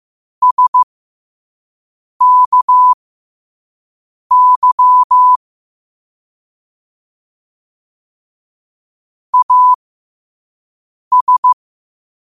Morse code, heard as a keyed tone, about 15 wpm, spaced as for 5 wpm, 1000 Hz. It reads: SKY AS
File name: qotdfiles / 5wpm